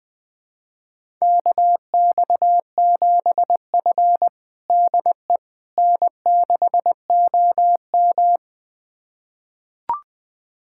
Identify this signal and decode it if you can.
Morse code, heard as a keyed tone, about 21 wpm, 700 Hz. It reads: KX7F DE N6OM
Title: USA Callsign DE Callsign - 4 Characters - 20wpm